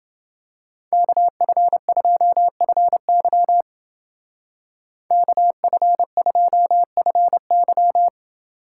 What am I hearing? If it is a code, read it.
Morse code, heard as a keyed tone, about 29 wpm, 700 Hz. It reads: KF2FY KF2FY